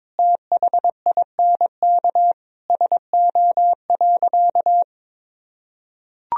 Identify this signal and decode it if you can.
Morse code, heard as a keyed tone, about 22 wpm, 700 Hz. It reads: THINK SO.
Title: Sentences from Top 1000 Words - Review Part 1 - 22wpm